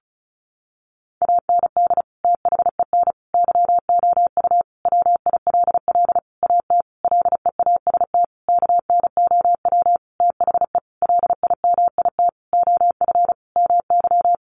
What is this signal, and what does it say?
Morse code, heard as a keyed tone, about 34 wpm, 700 Hz. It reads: AND THEN YOU WILL AT LEAST KNOW THE LIMIT OF MY
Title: Sentences from Top 1500 Words 35wpm